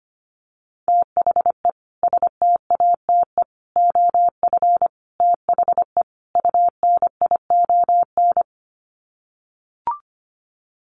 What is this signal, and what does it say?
Morse code, heard as a keyed tone, about 25 wpm, 700 Hz. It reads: THE STATE OF THE UNION
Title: Sets of 5 Words Encore 25wpm